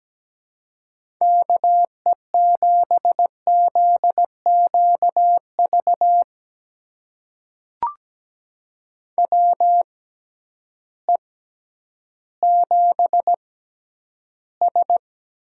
Morse code, heard as a keyed tone, about 17 wpm, spaced as for 10 wpm, 700 Hz. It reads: KE7ZQV WE7S
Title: USA Call Signs - 6x Character Spacing - 17wpm